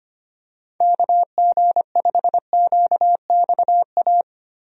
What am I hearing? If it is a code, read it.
Morse code, heard as a keyed tone, about 25 wpm, 700 Hz. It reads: KG5QXA
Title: USA Call Signs - 6 Characters - Repeated 3x - 25wpm